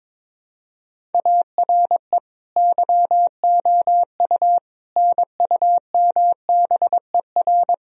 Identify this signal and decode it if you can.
Morse code, heard as a keyed tone, about 22 wpm, 700 Hz. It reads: ARE YOU NUMBER
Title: Sentences from Top 1000 Words - Review Part 1 - 22wpm